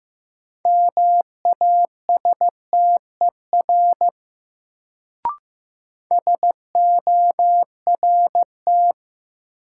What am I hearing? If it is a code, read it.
Morse code, heard as a keyed tone, about 15 wpm, 700 Hz. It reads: MASTER SORT
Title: Sentences from Top 1300 Words - New Word Drill Alt - 15wpm